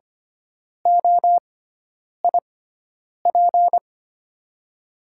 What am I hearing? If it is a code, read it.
Morse code, heard as a keyed tone, about 25 wpm, spaced as for 9 wpm, 700 Hz. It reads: OIP
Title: USA Call Signs - 5 Characters - 6x Character Spacing - 25wpm